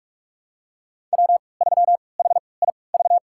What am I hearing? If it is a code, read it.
Morse code, heard as a keyed tone, about 46 wpm, spaced as for 25 wpm, 700 Hz. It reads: W3HIV
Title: USA Call Signs - 5 Characters - 3x Character Spacing - 45wpm